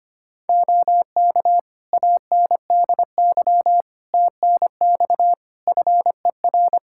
Morse code, heard as a keyed tone, about 25 wpm, 700 Hz. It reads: OK ANDY TNX FER
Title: K1USN SST QSO Practice 25wpm